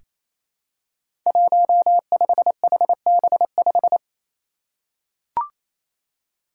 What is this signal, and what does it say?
Morse code, heard as a keyed tone, about 28 wpm, 700 Hz. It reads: 15HB5
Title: Five Challenging Letter-Number Combinations 28wpm